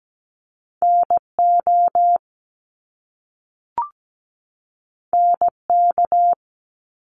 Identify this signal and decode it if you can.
Morse code, heard as a keyed tone, about 17 wpm, 700 Hz. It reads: NO NK
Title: Top Two Letter Combinations in Words 17wpm